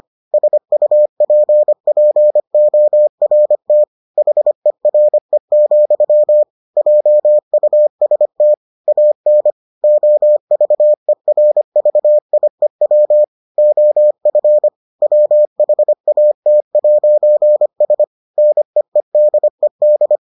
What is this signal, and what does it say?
Morse code, heard as a keyed tone, about 25 wpm, 600 Hz. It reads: SUPPORT HERE, JUST AN OVERVIEW OF WHAT'S NEEDED